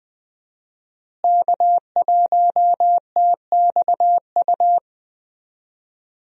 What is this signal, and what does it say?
Morse code, heard as a keyed tone, about 20 wpm, 700 Hz. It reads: K1TXU
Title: USA Call Signs - 5 Characters - Repeated 2x - 20wpm